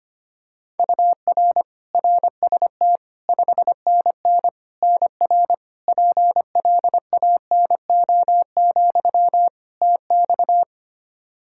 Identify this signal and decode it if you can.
Morse code, heard as a keyed tone, about 25 wpm, 700 Hz. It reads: UR RST 5NN NR PLANO, TX
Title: Practice QSO - Exchange 1 - Beginning - 25wpm